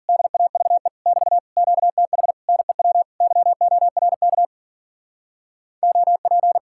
Morse code, heard as a keyed tone, about 48 wpm, 700 Hz. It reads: DAVE <BT> QTH NEW YORK OP
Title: Speed-Racing Common QSO Phrases Encore - 40wpm